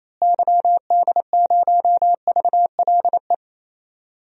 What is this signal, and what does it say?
Morse code, heard as a keyed tone, about 27 wpm, 700 Hz. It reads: YD0VLE
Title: DXCC-100 Call Signs - 6 Characters - Repeated 3x - 28wpm